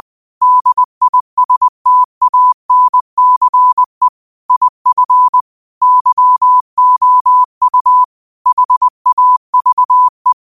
Morse code, heard as a keyed tone, about 20 wpm, 1000 Hz. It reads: DISTANCE IF YOU HAVE
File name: qotdfiles / 20wpm